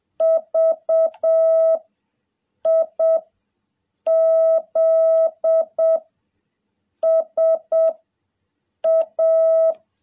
Morse code, heard as a keyed tone, about 7 wpm, 660 Hz. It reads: V I Z S A
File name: jh8cbh / 2021cw